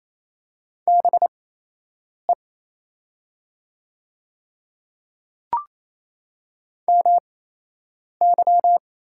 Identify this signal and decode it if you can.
Morse code, heard as a keyed tone, about 28 wpm, spaced as for 8 wpm, 700 Hz. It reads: BE MY
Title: 8x Character Spacing - Top 100 Words - 28wpm